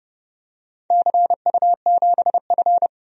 Morse code, heard as a keyed tone, about 32 wpm, 700 Hz. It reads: CU7F
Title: DXCC-100 Call Signs - 4 Characters - Repeated 2x - 30wpm